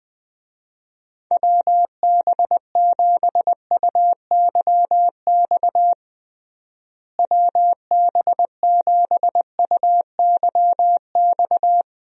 Morse code, heard as a keyed tone, about 20 wpm, 700 Hz. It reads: WB7UYX WB7UYX
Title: USA Call Signs - 6 Characters - Repeated 3x - 20wpm